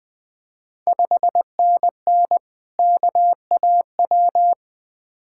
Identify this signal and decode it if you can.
Morse code, heard as a keyed tone, about 21 wpm, 700 Hz. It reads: 5NN KAW
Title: Ontario QSO Party Exchange 20wpm